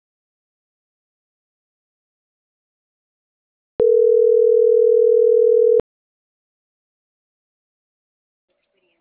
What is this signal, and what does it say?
Ringback tone (North America), 440 + 480 Hz.
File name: CENTRAL MAYAB / out lizbeth.mayab